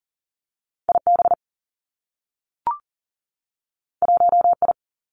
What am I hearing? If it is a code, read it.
Morse code, heard as a keyed tone, about 38 wpm, 700 Hz. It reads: I B 1 I
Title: Two Challenging Letter-Number Combinations 40wpm